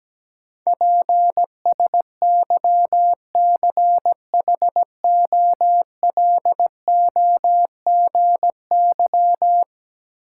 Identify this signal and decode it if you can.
Morse code, heard as a keyed tone, about 17 wpm, 700 Hz. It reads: PSYCHOLOGY
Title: Sentences from Top 1400 Words - New Word Drill Alt - 17wpm